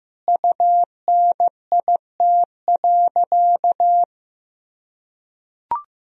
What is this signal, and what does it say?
Morse code, heard as a keyed tone, about 15 wpm, 700 Hz. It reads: UNIT.
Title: Sentences from Top 1000 Words - Review Part 2 - 15wpm